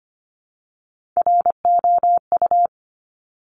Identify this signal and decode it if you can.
Morse code, heard as a keyed tone, about 25 wpm, 700 Hz. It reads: ROU
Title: Top Three Letter Combinations in Words 25wpm